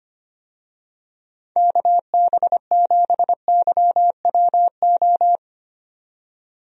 Morse code, heard as a keyed tone, about 25 wpm, 700 Hz. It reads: KB7YWO
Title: USA Call Signs - 6 Characters - 25wpm